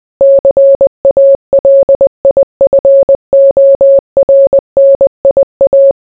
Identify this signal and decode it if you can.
Morse code, heard as a keyed tone, about 20 wpm, 550 Hz. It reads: CALIFORNIA